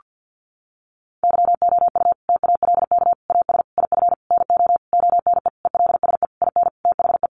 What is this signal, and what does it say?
Morse code, heard as a keyed tone, about 50 wpm, 700 Hz. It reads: YOU TALK AS IF NO ONE ELSE IN THE
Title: Sentences from Top 1100 Words 50wpm